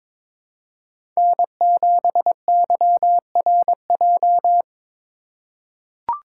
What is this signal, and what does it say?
Morse code, heard as a keyed tone, about 22 wpm, 700 Hz. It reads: N7YRJ
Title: USA Call Signs - 5 Characters - 3x Character Spacing - 22wpm